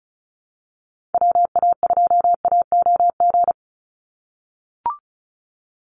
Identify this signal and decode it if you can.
Morse code, heard as a keyed tone, about 34 wpm, 700 Hz. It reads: WA2AOG
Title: Call Signs - 35wpm